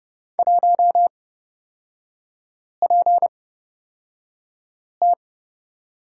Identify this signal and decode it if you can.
Morse code, heard as a keyed tone, about 30 wpm, spaced as for 5 wpm, 700 Hz. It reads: 1PT
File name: Lesson 36e - BT - Three Characters 30wpm